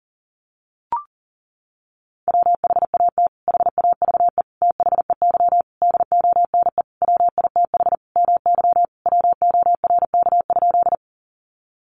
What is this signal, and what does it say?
Morse code, heard as a keyed tone, about 40 wpm, 700 Hz. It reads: WHAT HAVE THEY DONE WITH MY WORK?
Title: Sentences from Top 500 Words 40wpm